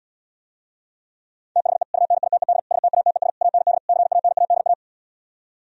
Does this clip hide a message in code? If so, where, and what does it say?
Yes: Morse code from 1.6 to 4.7 s, heard as a keyed tone, about 77 wpm, 700 Hz. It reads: THE UNITED STATES AND CANADA